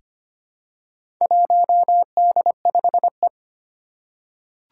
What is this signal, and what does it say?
Morse code, heard as a keyed tone, about 25 wpm, 700 Hz. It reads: 1D5E
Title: Four Challenging Letter-Number Combinations 25wpm